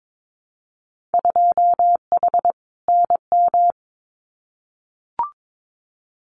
Morse code, heard as a keyed tone, about 22 wpm, 700 Hz. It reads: 2H NM
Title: Winter Field Day Exchanges Encore 22wpm